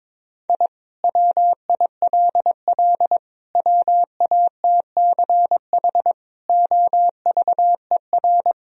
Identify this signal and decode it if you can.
Morse code, heard as a keyed tone, about 22 wpm, 700 Hz. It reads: I WILL WATCH OVER
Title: Sentences from Top 1000 Words - Review Part 1 - 22wpm